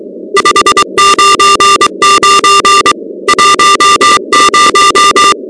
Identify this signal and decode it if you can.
Morse code, heard as a keyed tone, about 23 wpm, 1190 Hz, with noise. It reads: 599 10
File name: contests / 2019-cw-wpx-kt5j-k5pi- -2330 0-16